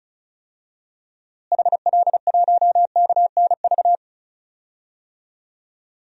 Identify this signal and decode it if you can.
Morse code, heard as a keyed tone, about 36 wpm, 700 Hz. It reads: HL1KNV
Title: DXCC-100 Call Signs - 6 Characters - Repeated 3x - 35wpm